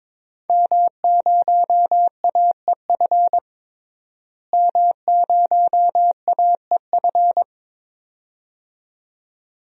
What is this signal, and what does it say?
Morse code, heard as a keyed tone, about 22 wpm, 700 Hz. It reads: M0AEF M0AEF
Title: DXCC-100 Call Signs - 5 Characters - Repeated 2x - 22wpm